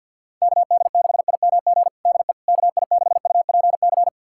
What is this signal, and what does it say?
Morse code, heard as a keyed tone, about 45 wpm, 700 Hz. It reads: KD6IMG DE KI6UPC
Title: USA Callsign DE Callsign 50wpm